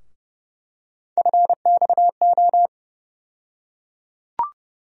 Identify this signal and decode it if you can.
Morse code, heard as a keyed tone, about 31 wpm, 700 Hz. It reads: FXO
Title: USA Call Sign Postfixes - 3 Letters 30wpm